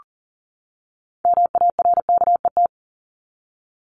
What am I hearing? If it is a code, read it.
Morse code, heard as a keyed tone, about 40 wpm, 700 Hz. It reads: MARKET
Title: ING Suffix - Top 100 Words with Base Words 40wpm